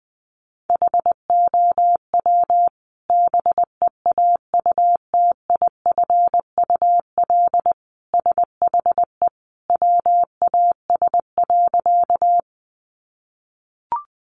Morse code, heard as a keyed tone, about 20 wpm, 700 Hz. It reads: HOW BEAUTIFUL SHE WAS.
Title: Sentences from Top 900 Words 20wpm